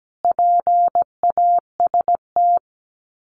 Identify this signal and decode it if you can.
Morse code, heard as a keyed tone, about 17 wpm, 700 Hz. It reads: PAST
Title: Sets of 3 Words 17wpm